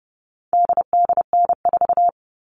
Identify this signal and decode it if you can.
Morse code, heard as a keyed tone, about 30 wpm, 700 Hz. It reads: DDN4
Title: All Cut Numbers - 4-Digit 30wpm